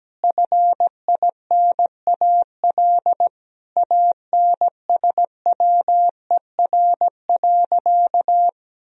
Morse code, heard as a keyed tone, about 17 wpm, 700 Hz. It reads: FINAL ANSWER.
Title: Sentences from Top 1000 Words - Review Part 2 - 17wpm